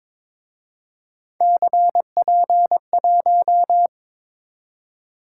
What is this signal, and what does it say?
Morse code, heard as a keyed tone, about 28 wpm, 700 Hz. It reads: CP1